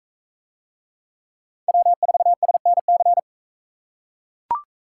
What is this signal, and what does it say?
Morse code, heard as a keyed tone, about 41 wpm, 700 Hz. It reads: W4SNC